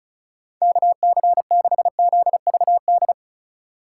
Morse code, heard as a keyed tone, about 34 wpm, 700 Hz. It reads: KC6ZVD
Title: USA Call Signs - 6 Characters - 4x Character Spacing - 35wpm